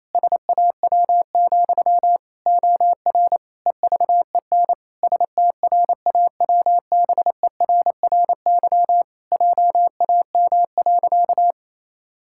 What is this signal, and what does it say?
Morse code, heard as a keyed tone, about 28 wpm, 700 Hz. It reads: SAW, OR EVEN STRAWBERRY JAM.
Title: Pangram Challenge v2 - 28wpm